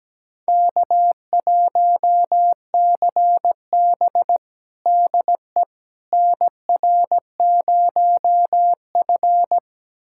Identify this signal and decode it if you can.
Morse code, heard as a keyed tone, about 17 wpm, 700 Hz. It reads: K1CB DE NR0F
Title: USA Callsign DE Callsign - 4 Characters - 17wpm